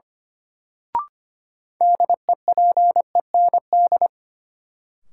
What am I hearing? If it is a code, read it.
Morse code, heard as a keyed tone, about 25 wpm, 700 Hz. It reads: DEPEND
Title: Sentences from Top 1300 Words - New Word Drill - 25wpm